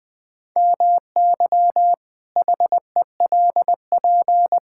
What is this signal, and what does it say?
Morse code, heard as a keyed tone, about 20 wpm, 700 Hz. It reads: MY HELP
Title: Pangram Challenge v2 - 20wpm